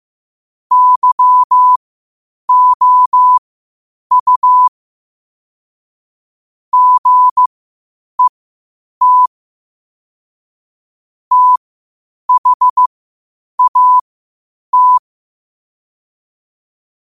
Morse code, heard as a keyed tone, about 15 wpm, spaced as for 8 wpm, 1000 Hz. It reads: YOU GET THAT
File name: qotdfiles / 8wpm